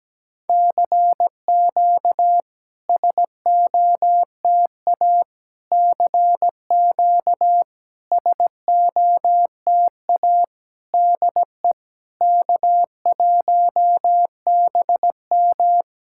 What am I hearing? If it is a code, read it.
Morse code, heard as a keyed tone, about 17 wpm, 700 Hz. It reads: CQ SOTA CQ SOTA DE K1BM
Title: SOTA QSO Practice 17wpm